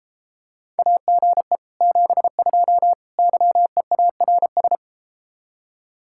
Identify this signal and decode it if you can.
Morse code, heard as a keyed tone, about 33 wpm, 700 Hz. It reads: AGE 72 YEARS